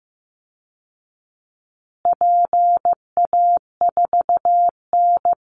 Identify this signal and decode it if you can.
Morse code, heard as a keyed tone, about 15 wpm, 700 Hz. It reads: PA4N
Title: Call Signs Repeated 2x - 15wpm